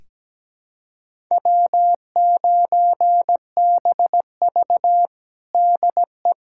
Morse code, heard as a keyed tone, about 17 wpm, 700 Hz. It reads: W9BV DE
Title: USA Callsign DE Callsign - 4 Characters - 17wpm